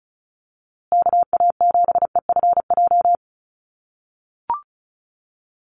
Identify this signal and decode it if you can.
Morse code, heard as a keyed tone, about 34 wpm, 700 Hz. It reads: KA7EFJ
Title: USA Call Signs Repeated 2x 35wpm